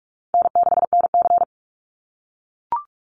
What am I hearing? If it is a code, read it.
Morse code, heard as a keyed tone, about 44 wpm, 700 Hz. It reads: N6NC
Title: Easy Call Signs - 45wpm